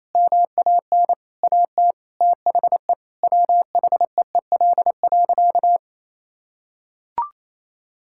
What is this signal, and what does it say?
Morse code, heard as a keyed tone, about 27 wpm, 700 Hz. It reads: MAN AT THE WHEEL.
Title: Sentences from Top 1000 Words - Review Part 5 - 28wpm